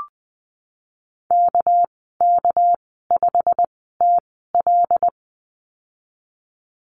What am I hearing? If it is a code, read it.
Morse code, heard as a keyed tone, about 20 wpm, spaced as for 15 wpm, 700 Hz. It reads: KK5TL
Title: 200 Lesson 32g - Letter k - Call Signs 20wpm